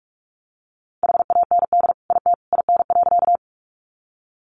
Morse code, heard as a keyed tone, about 44 wpm, 700 Hz. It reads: HAND IT IN.
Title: Sentences from Top 200 Words 45wpm